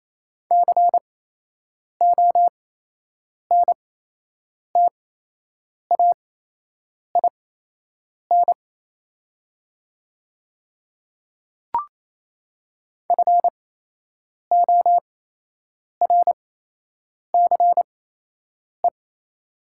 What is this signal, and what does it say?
Morse code, heard as a keyed tone, about 28 wpm, spaced as for 8 wpm, 700 Hz. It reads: CONTAIN FORCE
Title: 8x Character Spacing - Top 500 Words - 28wpm